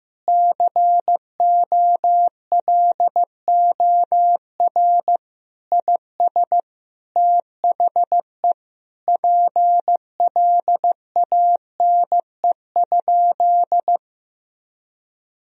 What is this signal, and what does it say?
Morse code, heard as a keyed tone, about 15 wpm, 700 Hz. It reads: COLOR IS THE PLANE?
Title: Sentences from Top 1000 Words - Review Part 3 - 15wpm